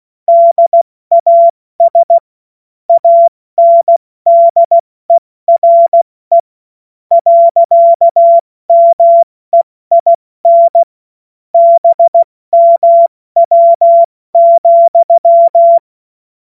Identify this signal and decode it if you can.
Morse code, heard as a keyed tone, about 16 wpm, spaced as for 13 wpm, 680 Hz. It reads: DAS ANDERE. MEIN BMW,
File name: Morsekurs / Witze / Witz 02. August 2022 70 F80